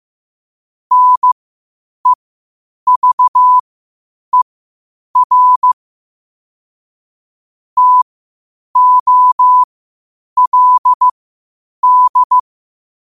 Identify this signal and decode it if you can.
Morse code, heard as a keyed tone, about 15 wpm, spaced as for 8 wpm, 1000 Hz. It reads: NEVER TOLD
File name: qotdfiles / 8wpm